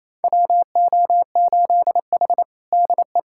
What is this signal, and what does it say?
Morse code, heard as a keyed tone, about 28 wpm, 690 Hz. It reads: WO8H DE
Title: USA Callsign DE Callsign - 4 Characters - 28wpm